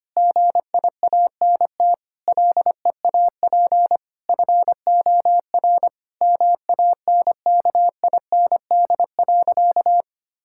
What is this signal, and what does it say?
Morse code, heard as a keyed tone, about 25 wpm, 700 Hz. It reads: GIANT LEAP FOR MANKIND.
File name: practice / 25wpm / Difficult Sentences with Hints - 25wpm